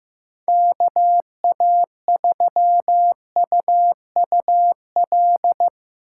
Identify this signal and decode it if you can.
Morse code, heard as a keyed tone, about 15 wpm, 700 Hz. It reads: KA3UUL
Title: USA Call Signs - 6 Characters - 8x Character Spacing - 15wpm